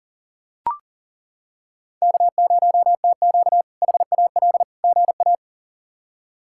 Morse code, heard as a keyed tone, about 40 wpm, 700 Hz. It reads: K0TQ HAL GA
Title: K1USN SST QSO Practice - 40wpm